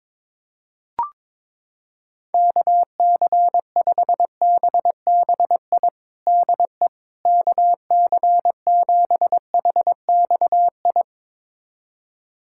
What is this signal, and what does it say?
Morse code, heard as a keyed tone, about 22 wpm, 700 Hz. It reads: KC5BBI DE KC7HXI
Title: USA Callsign DE Callsign - 6 Characters - 22wpm